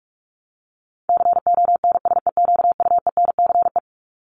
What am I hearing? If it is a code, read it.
Morse code, heard as a keyed tone, about 44 wpm, 700 Hz. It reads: CONSEQUENCE
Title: LY Suffix - Top 100 Words with Base Words 45wpm